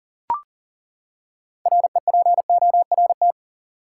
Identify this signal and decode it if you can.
Morse code, heard as a keyed tone, about 40 wpm, 700 Hz. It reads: REPORT
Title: Six Letter Words - 40wpm